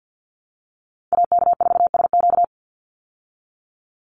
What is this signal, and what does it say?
Morse code, heard as a keyed tone, about 48 wpm, 700 Hz. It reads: AK4SQ
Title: Call Signs Repeated 2x - 50wpm